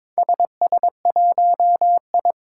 Morse code, heard as a keyed tone, about 22 wpm, 700 Hz. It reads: SS1I
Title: Four Challenging Letter-Number Combinations 22wpm